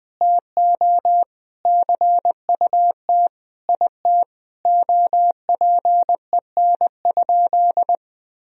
Morse code, heard as a keyed tone, about 20 wpm, 700 Hz. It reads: TO CUT IT OPEN?
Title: Sentences from Top 1000 Words - Review Part 1 - 20wpm